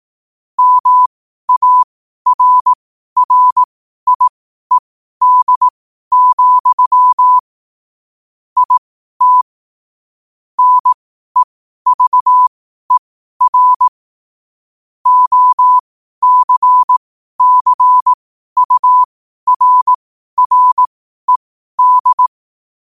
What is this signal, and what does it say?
Morse code, heard as a keyed tone, about 18 wpm, spaced as for 12 wpm, 1000 Hz. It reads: MARRIED, IT NEVER OCCURRED
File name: qotdfiles / 12wpm